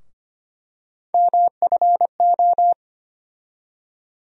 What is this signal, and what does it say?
Morse code, heard as a keyed tone, about 25 wpm, 700 Hz. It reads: MFO
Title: USA Call Sign Postfixes - 3 Letters 25wpm